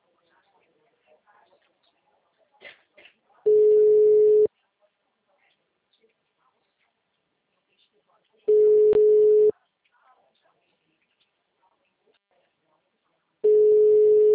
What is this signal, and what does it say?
Ringback tone (Europe, 425 Hz), with speech.